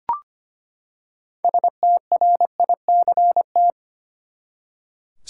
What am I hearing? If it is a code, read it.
Morse code, heard as a keyed tone, about 25 wpm, 700 Hz. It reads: STRICT
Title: Six Letter Words - 25wpm